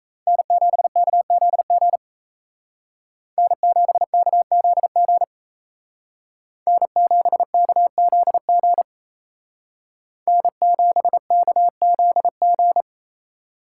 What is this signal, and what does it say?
Morse code, heard as a keyed tone, about 36 wpm, 700 Hz. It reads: N7KZG N7KZG N7KZG N7KZG